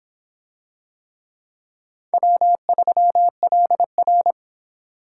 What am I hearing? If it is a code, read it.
Morse code, heard as a keyed tone, about 26 wpm, 700 Hz. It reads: W3LR